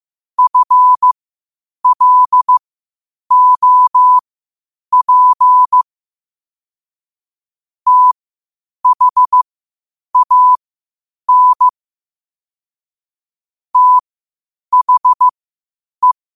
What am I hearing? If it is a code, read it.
Morse code, heard as a keyed tone, about 15 wpm, spaced as for 8 wpm, 1000 Hz. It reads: FLOP THAN THE
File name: qotdfiles / 8wpm